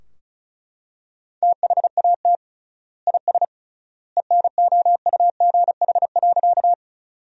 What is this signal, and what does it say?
Morse code, heard as a keyed tone, about 36 wpm, 700 Hz. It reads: THAT IS ENOUGH.
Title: 3x Word-Spacing - Sentences from Top 400 Words - 35wpm